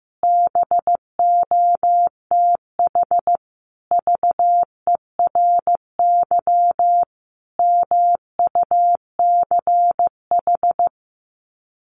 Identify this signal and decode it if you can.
Morse code, heard as a keyed tone, about 15 wpm, 700 Hz. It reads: BOTH VERY MUCH
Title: Sets of 5 Words 15wpm